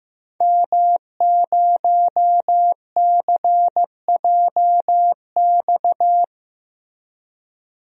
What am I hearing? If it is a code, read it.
Morse code, heard as a keyed tone, about 15 wpm, 700 Hz. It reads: M0CJX
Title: DXCC-100 Call Signs - 5 Characters - Repeated 3x - 15wpm